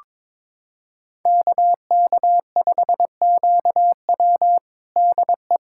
Morse code, heard as a keyed tone, about 22 wpm, 700 Hz. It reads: KK5QW DE
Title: USA Callsign DE Callsign - 5 Characters - 22wpm